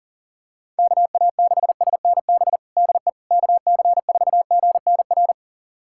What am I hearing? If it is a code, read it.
Morse code, heard as a keyed tone, about 37 wpm, 700 Hz. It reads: KA6SNB DE KC4GNR
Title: USA Callsign DE Callsign - 6 Characters - 40wpm